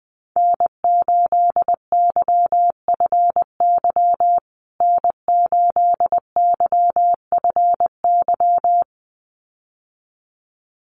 Morse code, heard as a keyed tone, about 20 wpm, 700 Hz. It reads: N8YFY N8YFY
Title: USA Call Signs Repeated 2x 20wpm